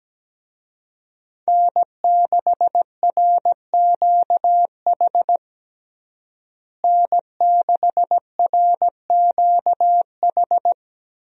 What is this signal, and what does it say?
Morse code, heard as a keyed tone, about 17 wpm, 700 Hz. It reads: N6RQH N6RQH
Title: USA Call Signs - 5 Characters - Repeated 3x - 17wpm